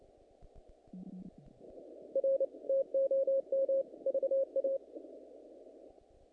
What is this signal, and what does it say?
Morse code, heard as a keyed tone, about 29 wpm, 540 Hz. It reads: R TOM VA
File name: contests / naqp-cw-k5tr-op-k5pi- -2100